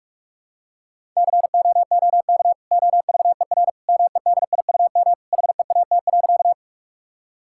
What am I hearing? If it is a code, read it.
Morse code, heard as a keyed tone, about 43 wpm, 700 Hz. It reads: COOK OVER MEDIUM HEAT.
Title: Sentences from Top 1000 Words - Review Part 6 - 45wpm